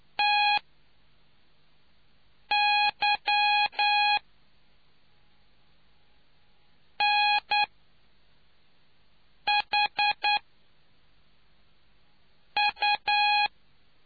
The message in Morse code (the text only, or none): TYNHU